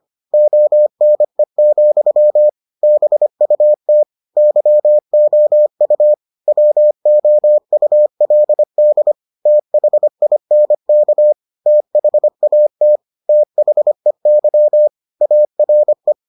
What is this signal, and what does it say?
Morse code, heard as a keyed tone, about 25 wpm, 600 Hz. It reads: ONE, BUT YOU WOULD THINK THAT THEY ARE